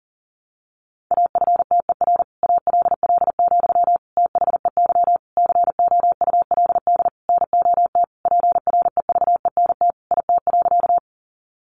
Morse code, heard as a keyed tone, about 40 wpm, 700 Hz. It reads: AFTER ALL, THEY COULD NOT PREVENT IT.
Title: Sentences from Top 1100 Words 40wpm